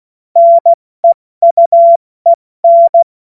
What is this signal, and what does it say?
Morse code, heard as a keyed tone, about 16 wpm, spaced as for 14 wpm, 680 Hz. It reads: NEUEN
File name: Morsekurs / Witze / Witz 05 März 2024 70 F80